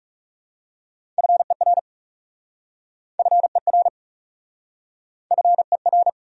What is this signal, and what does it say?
Morse code, heard as a keyed tone, about 40 wpm, 700 Hz. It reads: FER FER FER